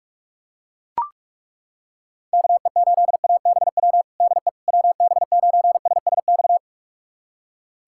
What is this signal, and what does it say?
Morse code, heard as a keyed tone, about 41 wpm, 700 Hz. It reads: KE8ABW DE WB9SSX
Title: USA Callsign DE Callsign - 6 Characters - 45wpm